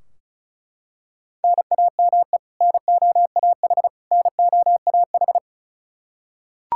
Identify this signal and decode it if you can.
Morse code, heard as a keyed tone, about 34 wpm, 700 Hz. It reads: NAME NOAH NOAH
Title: Practice QSO - Exchange 1 - Beginning - 35wpm